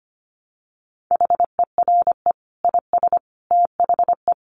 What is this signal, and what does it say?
Morse code, heard as a keyed tone, about 25 wpm, 700 Hz. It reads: HERE IS THE
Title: Sentences from Top 300 Words 25wpm